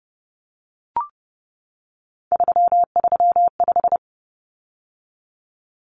Morse code, heard as a keyed tone, about 30 wpm, 700 Hz. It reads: 335
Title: Numbers - 3-Digit 30wpm